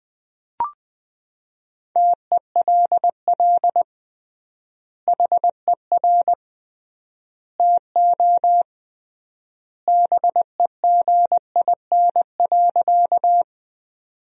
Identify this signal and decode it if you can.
Morse code, heard as a keyed tone, about 20 wpm, 700 Hz. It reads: TELL HER TO BEGIN.
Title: 3x Word-Spacing - Sentences from Top 300 Words - 20wpm